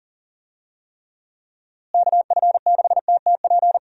Morse code, heard as a keyed tone, about 40 wpm, 700 Hz. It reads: KF6TTP
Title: USA Call Signs - 6 Characters - Repeated 3x - 40wpm